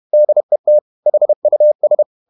Morse code, heard as a keyed tone, about 31 wpm, 600 Hz. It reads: DET HUS
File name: Ferie QRQ 31wpm